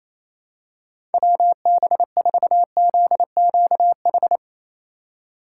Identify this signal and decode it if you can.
Morse code, heard as a keyed tone, about 28 wpm, 700 Hz. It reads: WB4ZQH
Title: USA Call Signs - 6 Characters - 4x Character Spacing - 28wpm